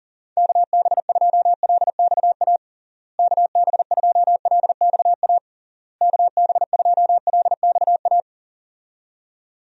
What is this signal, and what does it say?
Morse code, heard as a keyed tone, about 40 wpm, 700 Hz. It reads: KB2LXA KB2LXA KB2LXA
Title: USA Call Signs - 6 Characters - Repeated 3x - 40wpm